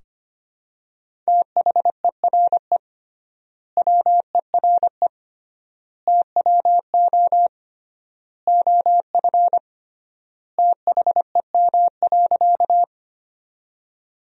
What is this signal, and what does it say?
Morse code, heard as a keyed tone, about 25 wpm, 700 Hz. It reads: THERE WERE TWO OF THEM.
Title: 3x Word-Spacing - Sentences from Top 100 Words - 25wpm